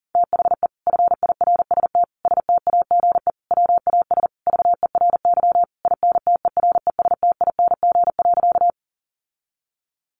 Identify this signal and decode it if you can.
Morse code, heard as a keyed tone, about 40 wpm, 700 Hz. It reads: THE FIRST STAGE WAS VERY INTERESTING.
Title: Sentences from Top 1200 Words 40wpm